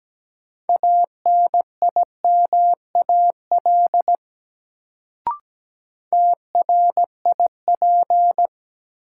Morse code, heard as a keyed tone, about 17 wpm, 700 Hz. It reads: ANIMAL TRIP
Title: Sentences from Top 1300 Words - New Word Drill Alt - 17wpm